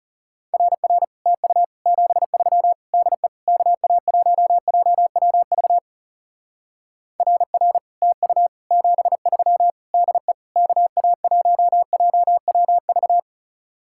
Morse code, heard as a keyed tone, about 36 wpm, 700 Hz. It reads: RR TU 73 DE KA1JWV RR TU 73 DE KA1JWV
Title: Speed-Racing Common QSO Phrases - 30wpm